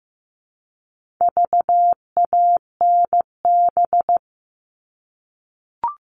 Morse code, heard as a keyed tone, about 15 wpm, 700 Hz. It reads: VANB
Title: All Cut Numbers - 4-Digit 15wpm